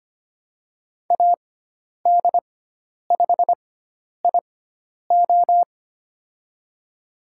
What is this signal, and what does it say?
Morse code, heard as a keyed tone, about 25 wpm, spaced as for 10 wpm, 700 Hz. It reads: AD5IO
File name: Lesson 16g - Letter c - Call Signs 25wpm